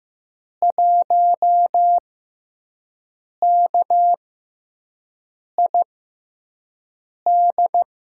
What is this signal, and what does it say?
Morse code, heard as a keyed tone, about 15 wpm, spaced as for 5 wpm, 700 Hz. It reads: 1KID
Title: USA Call Signs - 6x Character Spacing - 15wpm